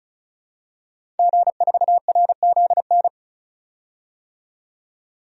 Morse code, heard as a keyed tone, about 34 wpm, 700 Hz. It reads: G4RZN
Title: DXCC-100 Call Signs - 5 Characters - 35wpm